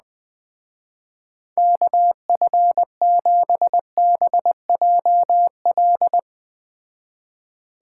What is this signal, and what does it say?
Morse code, heard as a keyed tone, about 20 wpm, 700 Hz. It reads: KF7BJL